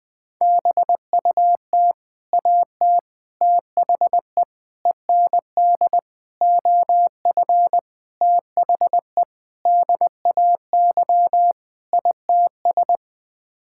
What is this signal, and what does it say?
Morse code, heard as a keyed tone, about 20 wpm, 700 Hz. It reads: BUT AT THE END OF THE DAY ITS
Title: Sets of 8 Words - 20wpm